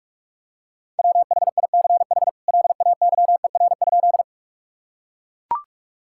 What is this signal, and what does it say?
Morse code, heard as a keyed tone, about 41 wpm, 700 Hz. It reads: WHICH LAYER?
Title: Sentences from Top 1600 Words 45wpm